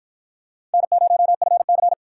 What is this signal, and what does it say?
Morse code, heard as a keyed tone, about 52 wpm, 700 Hz. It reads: N0FC